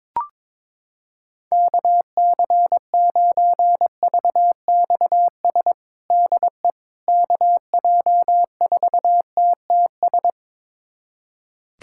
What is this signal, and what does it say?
Morse code, heard as a keyed tone, about 22 wpm, 700 Hz. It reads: KC9VXS DE KJ4TTS
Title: USA Callsign DE Callsign - 6 Characters - 22wpm